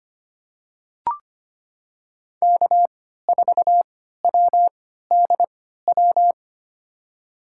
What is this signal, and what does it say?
Morse code, heard as a keyed tone, about 25 wpm, spaced as for 14 wpm, 700 Hz. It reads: K4WDW
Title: USA Call Signs - 5 Characters - 3x Character Spacing - 25wpm